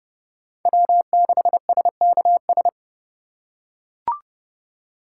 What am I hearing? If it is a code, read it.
Morse code, heard as a keyed tone, about 30 wpm, 700 Hz. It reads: W6SKS